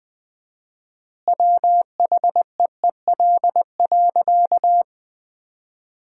Morse code, heard as a keyed tone, about 20 wpm, 700 Hz. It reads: WHEEL.
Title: 4x Word-Spacing - Sentences from Top 500 Words - 20wpm